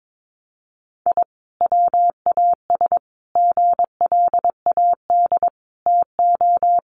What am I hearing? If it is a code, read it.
Morse code, heard as a keyed tone, about 22 wpm, 700 Hz. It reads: I WAS GLAD TO
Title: Sentences from Top 1100 Words 22wpm